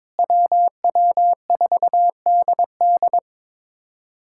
Morse code, heard as a keyed tone, about 22 wpm, 700 Hz. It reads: WW4DD
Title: USA Callsign DE Callsign - 5 Characters - 22wpm